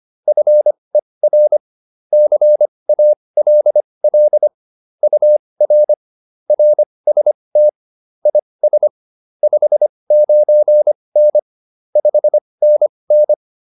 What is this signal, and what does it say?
Morse code, heard as a keyed tone, about 25 wpm, spaced as for 20 wpm, 600 Hz. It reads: FER CALL UR RST IS 59N 5NN